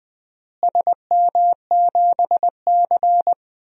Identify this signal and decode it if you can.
Morse code, heard as a keyed tone, about 19 wpm, 700 Hz. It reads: SM7C
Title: DXCC-100 Call Signs - 4 Characters - Repeated 3x - 20wpm